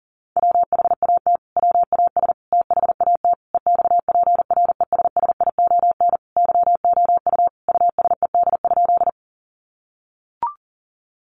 Morse code, heard as a keyed tone, about 37 wpm, 700 Hz. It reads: WHAT WAS THAT EXPRESSION YOU USED?
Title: Sentences from Top 1200 Words 40wpm